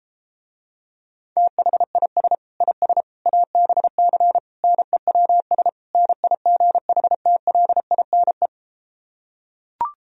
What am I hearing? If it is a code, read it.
Morse code, heard as a keyed tone, about 33 wpm, 700 Hz. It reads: THIS IS ABC NEWS NIGHTLINE